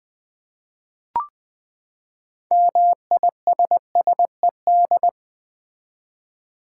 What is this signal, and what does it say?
Morse code, heard as a keyed tone, about 21 wpm, 700 Hz. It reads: MISSED
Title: Common QSO Phrases - 4x Word-Spacing - 20wpm